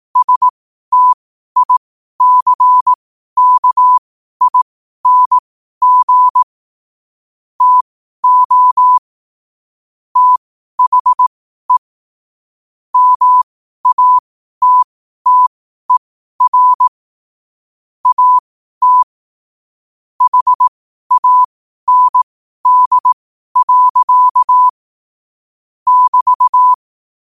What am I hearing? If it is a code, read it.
Morse code, heard as a keyed tone, about 18 wpm, spaced as for 12 wpm, 1000 Hz. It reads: STICKING TO THE MATTER AT HAND. <BT>